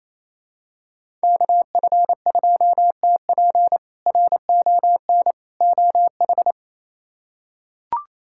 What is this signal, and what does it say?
Morse code, heard as a keyed tone, about 28 wpm, 700 Hz. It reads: KF2TP RON OH
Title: K1USN SST QSO Practice - 28wpm